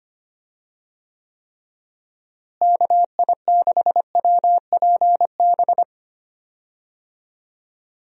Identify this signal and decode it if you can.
Morse code, heard as a keyed tone, about 25 wpm, 700 Hz. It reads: KI6WPB
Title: USA Call Signs - 6 Characters - Repeated 3x - 25wpm